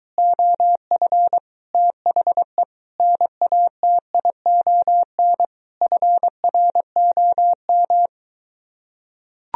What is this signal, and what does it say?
Morse code, heard as a keyed tone, about 23 wpm, 700 Hz. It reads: OF THE NATION FROM